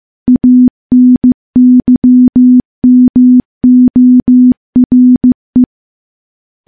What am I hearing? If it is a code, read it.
Morse code, heard as a keyed tone, about 15 wpm, 250 Hz. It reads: ANYMORE
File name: morse-training-15WPM